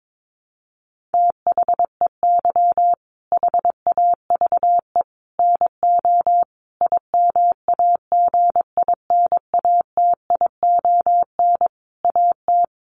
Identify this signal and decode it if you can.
Morse code, heard as a keyed tone, about 22 wpm, 700 Hz. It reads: THEY HAVE NO IMAGINATION AT
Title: Sentences from Top 1300 Words 22wpm